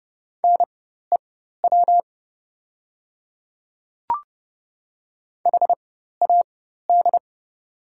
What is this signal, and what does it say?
Morse code, heard as a keyed tone, about 30 wpm, spaced as for 14 wpm, 700 Hz. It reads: NEW HAD
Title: 4x Character Spacing - Top 200 Words - 30wpm